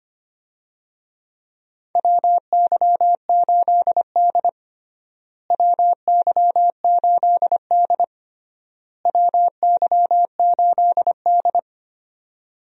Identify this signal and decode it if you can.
Morse code, heard as a keyed tone, about 25 wpm, 700 Hz. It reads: WY8D WY8D WY8D
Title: USA Call Signs - 4 Characters - Repeated 3x - 25wpm